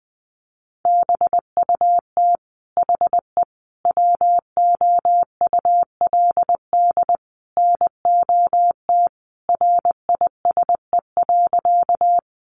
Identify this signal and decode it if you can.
Morse code, heard as a keyed tone, about 20 wpm, 700 Hz. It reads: BUT HE WOULD NOT RISE.
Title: Sentences from Top 900 Words 20wpm